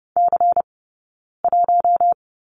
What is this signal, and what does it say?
Morse code, heard as a keyed tone, about 30 wpm, 700 Hz. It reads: C 1
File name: 092 Lesson 16d - Letter c - Two Characters 30wpm